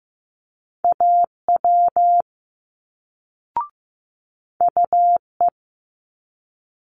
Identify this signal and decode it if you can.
Morse code, heard as a keyed tone, about 15 wpm, 700 Hz. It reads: AW UE
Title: Multi-letter Phonemes Quick Fire 15wpm